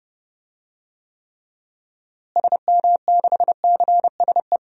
Morse code, heard as a keyed tone, about 31 wpm, 700 Hz. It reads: SM6CSE
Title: DXCC-100 Call Signs - 6 Characters - Repeated 2x - 30wpm